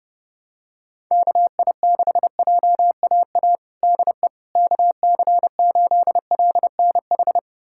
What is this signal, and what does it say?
Morse code, heard as a keyed tone, about 30 wpm, 700 Hz. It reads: KI6JAA DE KC8LNH
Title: USA Callsign DE Callsign 30wpm